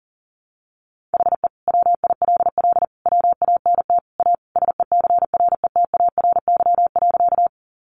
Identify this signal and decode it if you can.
Morse code, heard as a keyed tone, about 37 wpm, 700 Hz. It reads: HE WILL WANT A SECRETARY.
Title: Sentences from Top 1300 Words 40wpm